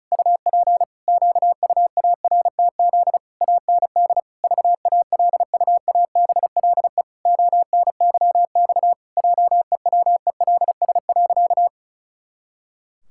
Morse code, heard as a keyed tone, about 36 wpm, 700 Hz. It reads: UP QUARTZ AND VALUABLE ONYX JEWELS.
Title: Pangram Challenge v2 - 35wpm